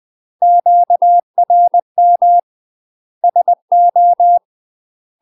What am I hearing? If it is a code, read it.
Morse code, heard as a keyed tone, about 20 wpm, 700 Hz. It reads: QRM SO
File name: UK MASTER QSOs TRAINING / 20WPM / UK MASTER QSOs 0048